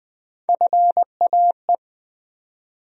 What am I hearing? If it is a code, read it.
Morse code, heard as a keyed tone, about 20 wpm, 700 Hz. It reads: FAE
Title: USA Call Sign Postfixes - 3 Letters 20wpm